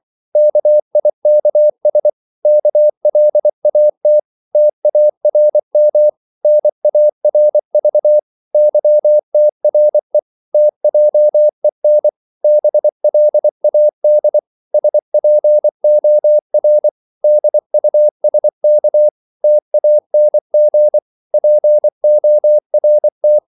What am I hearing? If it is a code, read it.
Morse code, heard as a keyed tone, about 24 wpm, 600 Hz. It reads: KIKS KLAT TARM NARV YTRE TJEN BLAD SPOR DUSK TANG PORT
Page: Korte danske ord 24wpm